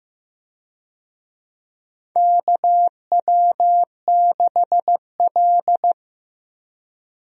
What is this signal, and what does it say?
Morse code, heard as a keyed tone, about 15 wpm, 700 Hz. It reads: KW6L
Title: USA Call Signs - 4 Characters - Repeated 2x - 15wpm